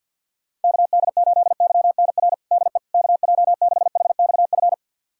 Morse code, heard as a keyed tone, about 45 wpm, 700 Hz. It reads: KD8YNR DE KJ6HXF
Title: USA Callsign DE Callsign 50wpm